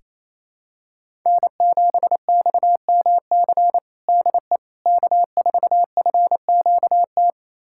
Morse code, heard as a keyed tone, about 28 wpm, 700 Hz. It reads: N7XMC DE K4FQT
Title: USA Callsign DE Callsign 28wpm